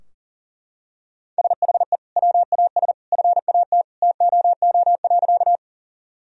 Morse code, heard as a keyed tone, about 40 wpm, 700 Hz. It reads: SHE WAS FAT TOO.
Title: Sentences from Top 1000 Words - Review Part 4 - 40wpm